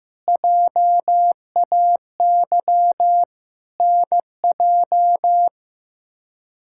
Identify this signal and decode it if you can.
Morse code, heard as a keyed tone, about 15 wpm, 700 Hz. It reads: JAY NJ
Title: K1USN SST QSO Practice - 15wpm